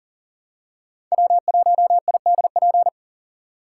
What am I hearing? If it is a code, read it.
Morse code, heard as a keyed tone, about 40 wpm, 700 Hz. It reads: W1IDP
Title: USA Call Signs - 5 Characters - 4x Character Spacing - 40wpm